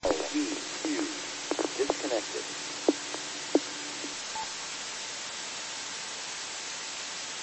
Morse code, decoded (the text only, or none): none